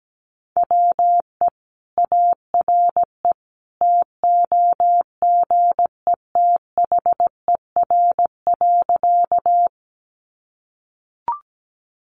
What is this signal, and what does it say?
Morse code, heard as a keyed tone, about 17 wpm, 700 Hz. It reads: WE ARE TOGETHER.
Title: Sentences from Top 300 Words 17wpm